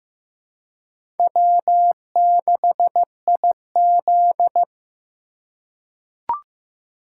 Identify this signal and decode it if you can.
Morse code, heard as a keyed tone, about 15 wpm, 700 Hz. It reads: W6IZ
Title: USA Call Signs - 4 Characters - 2x Character Spacing - 15wpm